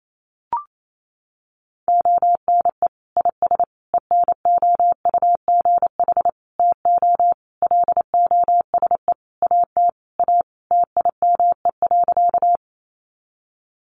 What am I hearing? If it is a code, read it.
Morse code, heard as a keyed tone, about 28 wpm, 700 Hz. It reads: ONE IS ENOUGH TO LOSE AT A TIME.
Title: Sentences from Top 1100 Words 28wpm